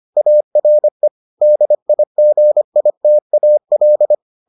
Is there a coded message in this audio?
Yes: Morse code from 0.2 to 4.1 s, heard as a keyed tone, about 25 wpm, 600 Hz. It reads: ARE DIGITAL